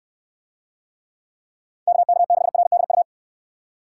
Morse code, heard as a keyed tone, about 67 wpm, 700 Hz. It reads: KK6RDU